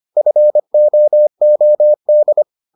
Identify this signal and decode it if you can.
Morse code, heard as a keyed tone, about 25 wpm, 600 Hz. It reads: FOOD